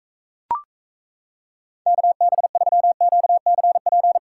Morse code, heard as a keyed tone, about 37 wpm, 700 Hz. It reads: KB3QCP